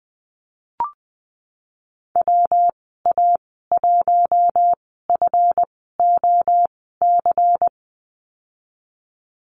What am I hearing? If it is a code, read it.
Morse code, heard as a keyed tone, about 20 wpm, spaced as for 15 wpm, 700 Hz. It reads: WA1FOC
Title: 154 Lesson 25g - Letter p - Call Signs 20wpm